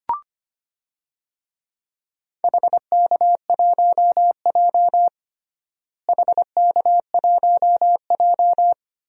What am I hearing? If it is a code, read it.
Morse code, heard as a keyed tone, about 25 wpm, 700 Hz. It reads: HK1J HK1J
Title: DXCC-100 Call Signs - 4 Characters - Repeated 2x - 25wpm